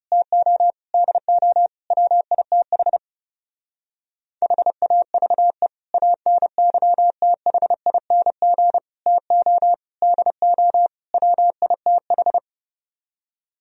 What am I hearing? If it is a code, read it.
Morse code, heard as a keyed tone, about 29 wpm, 700 Hz. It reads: TO DO WITH HAVE ANYTHING TO DO WITH
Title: Speed-Racing - Sets of 5 Words 30wpm